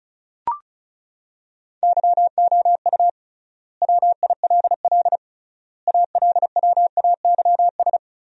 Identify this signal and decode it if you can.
Morse code, heard as a keyed tone, about 36 wpm, 700 Hz. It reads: YOU WILL ALWAYS
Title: 3x Word-Spacing - Sentences from Top 400 Words - 35wpm